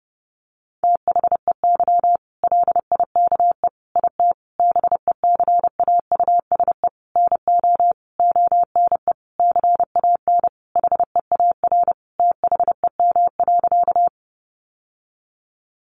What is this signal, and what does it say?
Morse code, heard as a keyed tone, about 30 wpm, 700 Hz. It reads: THEY LIKE IT BECAUSE NO ONE CAN HEAR THEM.
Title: Sentences from Top 600 Words 30wpm